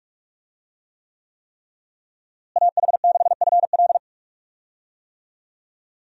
Morse code, heard as a keyed tone, about 42 wpm, 700 Hz. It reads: AH6FL
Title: USA Call Signs - 5 Characters - Repeated 3x - 45wpm